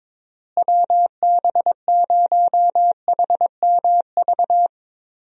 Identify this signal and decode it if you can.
Morse code, heard as a keyed tone, about 22 wpm, 700 Hz. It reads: WB0HMV
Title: USA Call Signs - 6 Characters - Repeated 3x - 22wpm